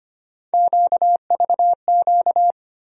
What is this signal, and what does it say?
Morse code, heard as a keyed tone, about 25 wpm, 700 Hz. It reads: QVQ
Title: USA Call Sign Postfixes - 3 Letters 25wpm